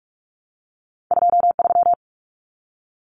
Morse code, heard as a keyed tone, about 46 wpm, 700 Hz. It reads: 23
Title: Common Cut Numbers - 2-Digit 45wpm